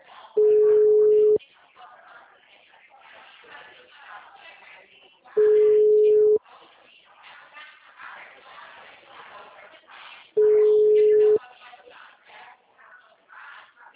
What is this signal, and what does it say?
Ringback tone (Europe, 425 Hz), with speech.